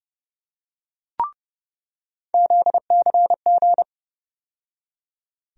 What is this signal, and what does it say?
Morse code, heard as a keyed tone, about 30 wpm, 700 Hz. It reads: ZCG
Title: USA Call Sign Postfixes - 3 Letters 30wpm